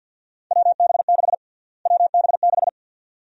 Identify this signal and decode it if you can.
Morse code, heard as a keyed tone, about 51 wpm, 700 Hz. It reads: WB6 WB6
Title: USA Call Sign Prefixes - Repeated 2x - 50wpm